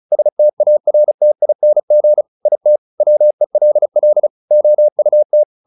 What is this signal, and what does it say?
Morse code, heard as a keyed tone, about 34 wpm, 600 Hz. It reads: STARTING IT WELL OUT
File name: the old man and the sea 2 35wpm